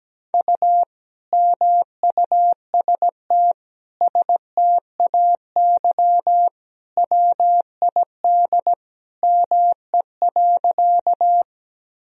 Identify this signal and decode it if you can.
Morse code, heard as a keyed tone, about 17 wpm, 700 Hz. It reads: U MUST STAY WID ME.
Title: Sentences from Top 500 Words and Abbreviations 17wpm